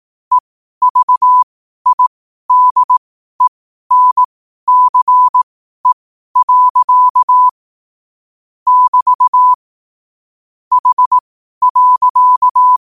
Morse code, heard as a keyed tone, about 18 wpm, spaced as for 12 wpm, 1000 Hz. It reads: EVIDENCE. <BT> H.
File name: qotdfiles / 12wpm